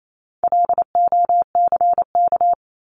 Morse code, heard as a keyed tone, about 28 wpm, 700 Hz. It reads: LOCK
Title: Top 1001-1250 Words 28wpm